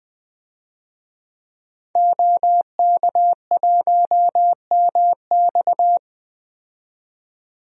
Morse code, heard as a keyed tone, about 21 wpm, 700 Hz. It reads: OK1MX
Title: DXCC-100 Call Signs - 5 Characters - 20wpm